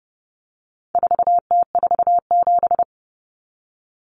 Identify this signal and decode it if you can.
Morse code, heard as a keyed tone, about 30 wpm, 700 Hz. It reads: 4T47
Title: Common Cut Numbers - 4-Digit 30wpm